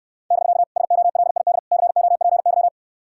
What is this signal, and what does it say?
Morse code, heard as a keyed tone, about 69 wpm, 700 Hz. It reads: <BK> SOLID COPY